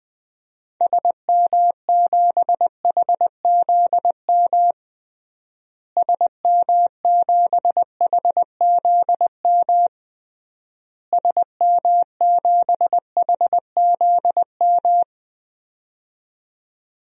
Morse code, heard as a keyed tone, about 20 wpm, 700 Hz. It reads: SM7HZM SM7HZM SM7HZM
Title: DXCC-100 Call Signs - 6 Characters - Repeated 3x - 20wpm